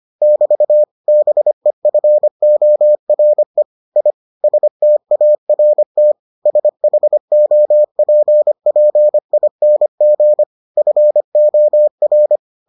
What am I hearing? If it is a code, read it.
Morse code, heard as a keyed tone, about 25 wpm, 600 Hz. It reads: <BT> BEFORE I START SHOPPING FOR